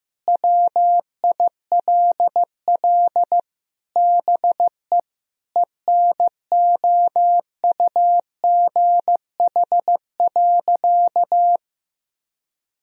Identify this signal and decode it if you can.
Morse code, heard as a keyed tone, about 16 wpm, 700 Hz. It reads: WILL BE ENOUGH.
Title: Sentences from Top 1000 Words - Review Part 2 - 15wpm